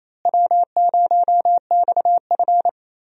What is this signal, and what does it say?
Morse code, heard as a keyed tone, about 28 wpm, 700 Hz. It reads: W0XF
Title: USA Call Signs - 4 Characters - Repeated 3x - 28wpm